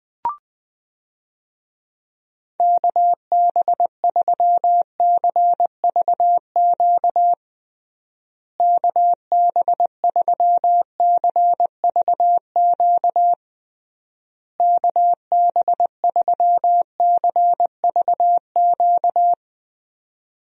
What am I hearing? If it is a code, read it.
Morse code, heard as a keyed tone, about 20 wpm, 700 Hz. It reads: KB3CVQ KB3CVQ KB3CVQ
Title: USA Call Signs - 6 Characters - Repeated 3x - 20wpm